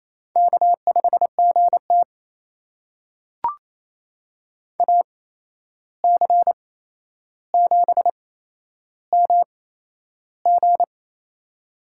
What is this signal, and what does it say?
Morse code, heard as a keyed tone, about 28 wpm, spaced as for 11 wpm, 700 Hz. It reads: K5GT AC7MG
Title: USA Call Signs - 8x Character Spacing - 28wpm